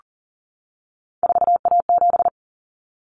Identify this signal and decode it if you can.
Morse code, heard as a keyed tone, about 40 wpm, 700 Hz. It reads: 4A7
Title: Common Cut Numbers - 3-Digit 40wpm